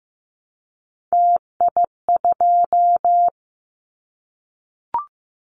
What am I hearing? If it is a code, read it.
Morse code, heard as a keyed tone, about 15 wpm, 700 Hz. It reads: TI2
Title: Callsign Prefixes for Top 100 DXCC - 15wpm